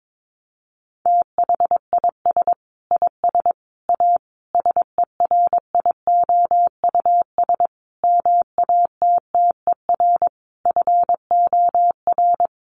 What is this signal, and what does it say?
Morse code, heard as a keyed tone, about 22 wpm, 700 Hz. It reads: THIS IS A SERIOUS MATTER FOR
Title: Sentences from Top 1100 Words 22wpm